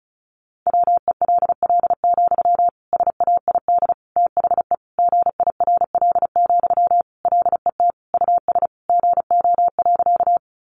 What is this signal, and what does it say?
Morse code, heard as a keyed tone, about 34 wpm, 700 Hz. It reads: WELL, SAID THE GIRL, LET US GO.
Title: Sentences from Top 400 Words 35wpm